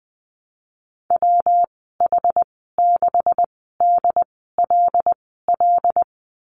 Morse code, heard as a keyed tone, about 20 wpm, spaced as for 14 wpm, 700 Hz. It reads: WH6DLL
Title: 173 Lesson 28g - Number 9 - Call Signs 20wpm